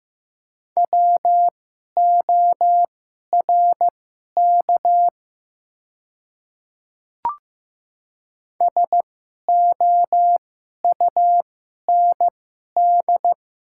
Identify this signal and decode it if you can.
Morse code, heard as a keyed tone, about 15 wpm, spaced as for 11 wpm, 700 Hz. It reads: WORK SOUND
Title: 2x Character Spacing - Top 200 Words - 15wpm